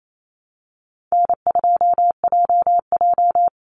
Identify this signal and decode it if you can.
Morse code, heard as a keyed tone, about 28 wpm, 700 Hz. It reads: N2JJ
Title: Call Signs - 28wpm